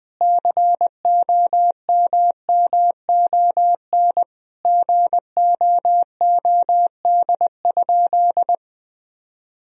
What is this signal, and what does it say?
Morse code, heard as a keyed tone, about 21 wpm, 700 Hz. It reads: COMMON GOOD?
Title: Sentences from Top 1000 Words - Review Part 3 - 20wpm